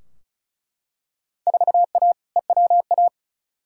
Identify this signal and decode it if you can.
Morse code, heard as a keyed tone, about 34 wpm, 700 Hz. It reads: 4A EWA
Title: Field Day Exchanges - 35wpm